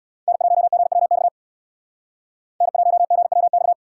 Morse code, heard as a keyed tone, about 68 wpm, 700 Hz. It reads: N1GWX N1GWX